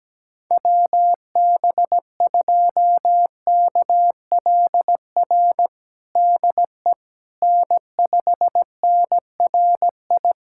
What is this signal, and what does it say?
Morse code, heard as a keyed tone, about 17 wpm, 700 Hz. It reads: WB2KLR DE N5NRI
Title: USA Callsign DE Callsign 17wpm